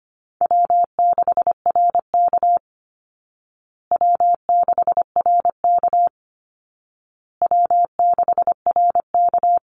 Morse code, heard as a keyed tone, about 25 wpm, 700 Hz. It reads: W6RK W6RK W6RK
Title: Call Signs Repeated 2x - 25wpm